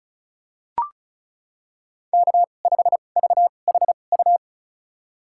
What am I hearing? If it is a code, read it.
Morse code, heard as a keyed tone, about 34 wpm, spaced as for 25 wpm, 700 Hz. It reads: K5VHU
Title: USA Call Signs - 2x Character Spacing - 35wpm